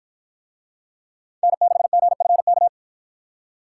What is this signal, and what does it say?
Morse code, heard as a keyed tone, about 53 wpm, 700 Hz. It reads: N6GFK